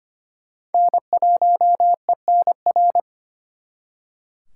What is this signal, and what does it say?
Morse code, heard as a keyed tone, about 33 wpm, 700 Hz. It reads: N1ENR